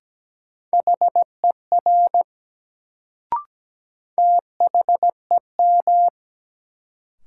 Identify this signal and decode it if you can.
Morse code, heard as a keyed tone, about 17 wpm, 700 Hz. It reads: HER THEM
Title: Sentences from Top 100 Words - New Word Drill - 17wpm